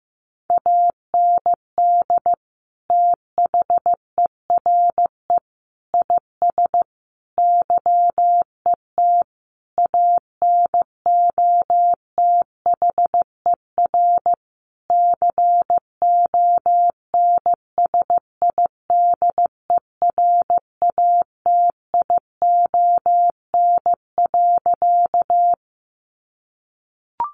Morse code, heard as a keyed tone, about 15 wpm, 700 Hz. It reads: AND THERE IS YET ANOTHER CONSIDERATION.
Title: Sentences from Top 1500 Words 15wpm